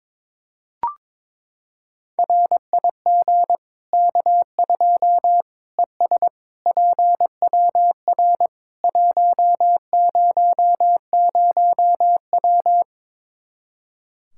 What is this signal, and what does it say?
Morse code, heard as a keyed tone, about 22 wpm, 700 Hz. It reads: RIG K2 ES PWR 100W
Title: Practice QSO - Exchange 2 - Info - 22wpm